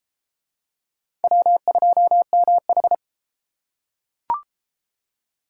Morse code, heard as a keyed tone, about 30 wpm, 700 Hz. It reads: W2MH